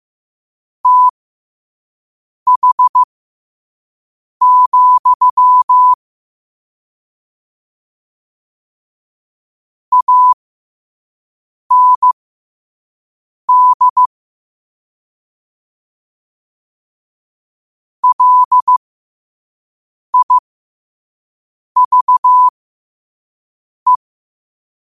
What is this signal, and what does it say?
Morse code, heard as a keyed tone, about 15 wpm, spaced as for 5 wpm, 1000 Hz. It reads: TH, AND LIVE
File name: qotdfiles / 5wpm